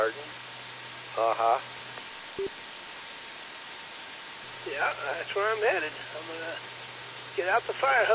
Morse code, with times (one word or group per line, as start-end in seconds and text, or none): none